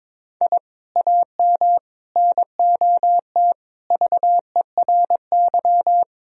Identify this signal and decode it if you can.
Morse code, heard as a keyed tone, about 22 wpm, 700 Hz. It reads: I AM NOT VERY
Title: Sentences from Top 1000 Words - Review Part 4 - 22wpm